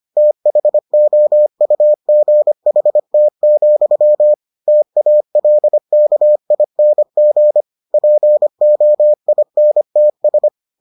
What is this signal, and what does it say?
Morse code, heard as a keyed tone, about 25 wpm, 600 Hz. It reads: THOUGHT, TALKING POINTS